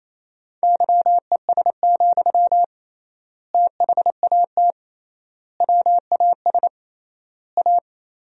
Morse code, heard as a keyed tone, about 28 wpm, 700 Hz. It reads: YES, THAT WAS A
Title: 3x Word-Spacing - Sentences from Top 500 Words - 28wpm